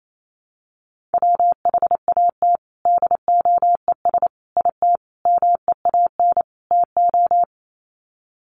Morse code, heard as a keyed tone, about 28 wpm, 700 Hz. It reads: WHAT DOES IT MEAN TO
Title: Sets of 5 Words Encore2 28wpm